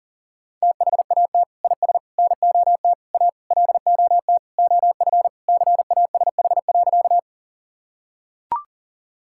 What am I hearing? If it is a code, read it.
Morse code, heard as a keyed tone, about 37 wpm, 700 Hz. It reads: THAT IS NOT A LOT OF CASH.
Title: Sentences from Top 1000 Words - Review Part 5 - 40wpm